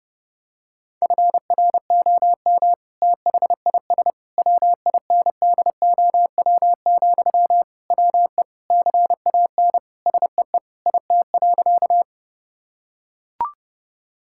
Morse code, heard as a keyed tone, about 30 wpm, 700 Hz. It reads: FROM THIS WINDOW, WE CAN SEE IT.
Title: Sentences from Top 1600 Words 30wpm